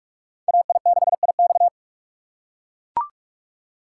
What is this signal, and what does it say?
Morse code, heard as a keyed tone, about 44 wpm, 700 Hz. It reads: AI6IX